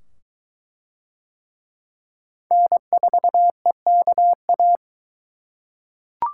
Morse code, heard as a keyed tone, about 22 wpm, 700 Hz. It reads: N4EKA